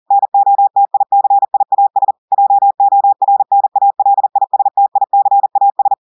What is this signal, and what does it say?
Morse code, heard as a keyed tone, about 40 wpm, 800 Hz. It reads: NOTICIAS JORNALISTICAS
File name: audios / portugais / portugais texte en clair 2 vitesse 40 mots minutes